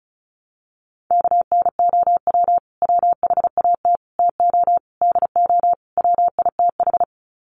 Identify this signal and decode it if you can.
Morse code, heard as a keyed tone, about 34 wpm, 700 Hz. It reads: KNOW WHAT TO DO WITH
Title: Sets of 5 Words 35wpm